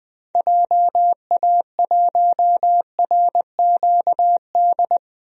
Morse code, heard as a keyed tone, about 20 wpm, 700 Hz. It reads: JA1RQD
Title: DXCC-100 Call Signs - 6 Characters - Repeated 3x - 20wpm